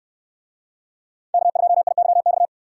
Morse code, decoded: K2IYX

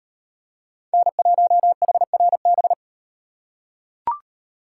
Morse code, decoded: N1HRB

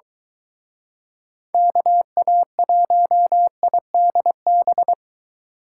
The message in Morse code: KA1IDB

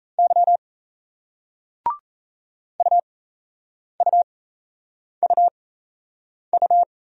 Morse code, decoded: Y UUUU

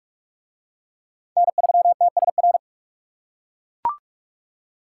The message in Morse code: N3TSR

